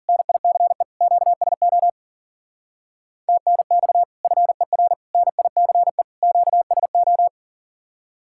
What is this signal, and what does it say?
Morse code, heard as a keyed tone, about 52 wpm, 700 Hz. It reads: NICE QSO TNX FER NICE QSO